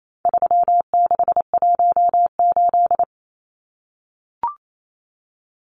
Morse code, heard as a keyed tone, about 28 wpm, 700 Hz. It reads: 3618
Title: Numbers - 4-Digit 28wpm